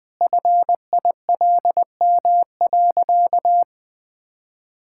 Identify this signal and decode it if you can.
Morse code, heard as a keyed tone, about 20 wpm, 700 Hz. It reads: FILM.
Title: Easy Sentences with Hints - 20wpm